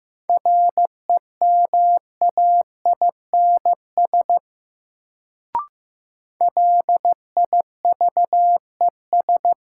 Morse code, heard as a keyed tone, about 15 wpm, 700 Hz. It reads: REMAINS LIVES
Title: Sets of 5 Words Encore - New Word Drill - 15wpm